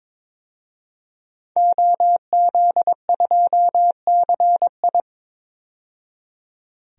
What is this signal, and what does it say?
Morse code, heard as a keyed tone, about 22 wpm, 700 Hz. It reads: OZ2CI
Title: DXCC-100 Call Signs - 5 Characters - 22wpm